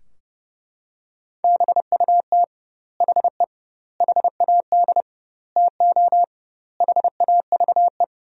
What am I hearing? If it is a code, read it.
Morse code, heard as a keyed tone, about 32 wpm, 700 Hz. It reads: BUT HE HAD TO HAVE
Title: 2x Word-Spacing - Sentences from Top 100 Words - 30wpm